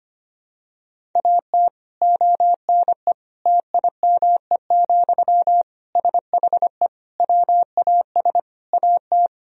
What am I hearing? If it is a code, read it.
Morse code, heard as a keyed tone, about 25 wpm, 700 Hz. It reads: AT ONE TIME, SHE WAS AT
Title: Sentences from Top 1000 Words - Review Part 1 - 25wpm